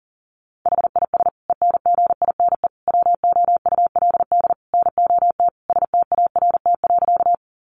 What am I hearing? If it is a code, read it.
Morse code, heard as a keyed tone, about 40 wpm, 700 Hz. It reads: HIS ENGINE WOULD NOT START.
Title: Sentences from Top 1500 Words 40wpm